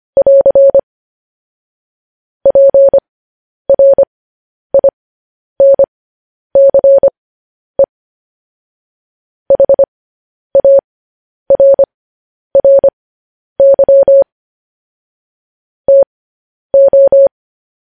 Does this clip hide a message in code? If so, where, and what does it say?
Yes: Morse code from 0.2 to 17.3 s, heard as a keyed tone, about 25 wpm, spaced as for 10 wpm, 550 Hz. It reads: <AR> PRINCE HARRY TO